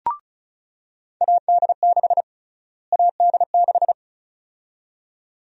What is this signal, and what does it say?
Morse code, heard as a keyed tone, about 34 wpm, 700 Hz. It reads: AD6 AD6
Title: USA Call Sign Prefixes - Repeated 2x - 35wpm